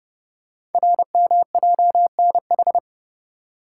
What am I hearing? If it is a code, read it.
Morse code, heard as a keyed tone, about 30 wpm, 700 Hz. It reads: RMJNH
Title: Code Groups - 5 Letters - 30wpm